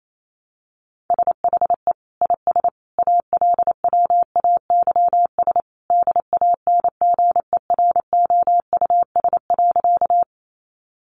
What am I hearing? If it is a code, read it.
Morse code, heard as a keyed tone, about 28 wpm, 700 Hz. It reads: SHE IS ALWAYS DANGEROUS.
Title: Sentences from Top 1200 Words 28wpm